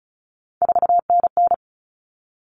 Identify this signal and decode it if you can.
Morse code, heard as a keyed tone, about 34 wpm, 700 Hz. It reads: 4NN
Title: Common Cut Numbers - 3-Digit 35wpm